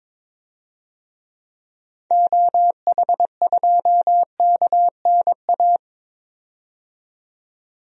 Morse code, heard as a keyed tone, about 22 wpm, 700 Hz. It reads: OH2KNA
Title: DXCC-100 Call Signs - 6 Characters - 22wpm